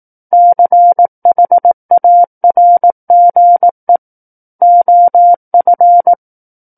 Morse code, heard as a keyed tone, about 18 wpm, 700 Hz. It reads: CHARGE OF
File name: 18 WPM